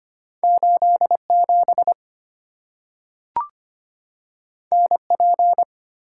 Morse code, heard as a keyed tone, about 25 wpm, 700 Hz. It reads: 8 7 N P
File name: Two Letter-Number Combinations - 25wpm